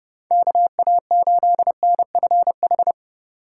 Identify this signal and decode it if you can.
Morse code, heard as a keyed tone, about 30 wpm, 700 Hz. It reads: KA8NFH
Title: USA Call Signs - 4x Character Spacing - 30wpm